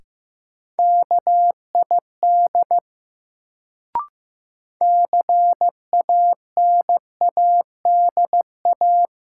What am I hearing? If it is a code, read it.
Morse code, heard as a keyed tone, about 15 wpm, 700 Hz. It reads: KID CANADA
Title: Sets of 5 Words - New Word Drill - 15wpm